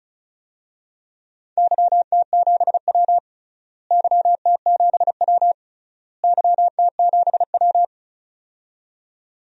Morse code, heard as a keyed tone, about 36 wpm, 700 Hz. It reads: YT7W YT7W YT7W
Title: DXCC-100 Call Signs - 4 Characters - Repeated 3x - 35wpm